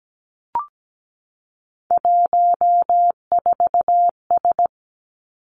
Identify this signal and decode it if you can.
Morse code, heard as a keyed tone, about 17 wpm, 700 Hz. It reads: 14S
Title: Three Challenging Letter-Number Combinations 17wpm